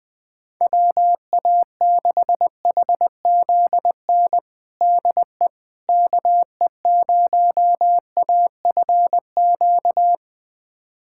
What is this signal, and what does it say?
Morse code, heard as a keyed tone, about 20 wpm, 700 Hz. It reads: WA6HZN DE KE0AFQ
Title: USA Callsign DE Callsign - 6 Characters - 20wpm